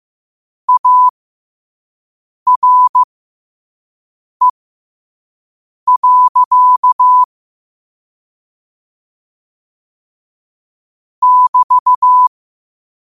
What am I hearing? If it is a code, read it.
Morse code, heard as a keyed tone, about 15 wpm, spaced as for 5 wpm, 1000 Hz. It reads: ARE. <BT>